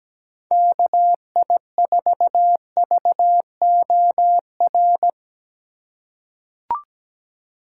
Morse code, heard as a keyed tone, about 17 wpm, 700 Hz. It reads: KI4VOR
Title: USA Call Signs - 4x Character Spacing - 17wpm